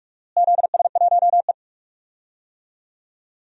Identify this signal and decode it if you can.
Morse code, heard as a keyed tone, about 46 wpm, 700 Hz. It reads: ZS1E
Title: DXCC-100 Call Signs - 4 Characters - Repeated 3x - 45wpm